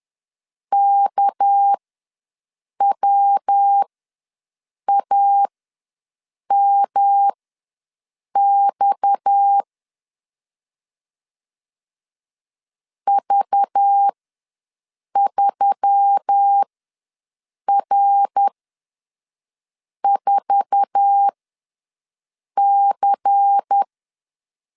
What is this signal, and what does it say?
Morse code, heard as a keyed tone, about 10 wpm, spaced as for 6 wpm, 800 Hz. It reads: KWAMX V3R4C
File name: LEZIONI / 8 YJ45 / 8 YJ4 ws 10wpm 4020